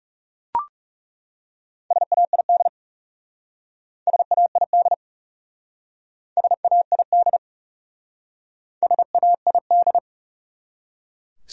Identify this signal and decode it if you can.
Morse code, heard as a keyed tone, about 38 wpm, 700 Hz. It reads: SAID SAID SAID SAID